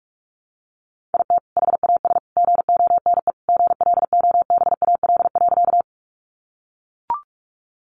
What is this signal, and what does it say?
Morse code, heard as a keyed tone, about 44 wpm, 700 Hz. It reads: IT HAS GONE GLOBAL.
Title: Sentences from Top 1200 Words 45wpm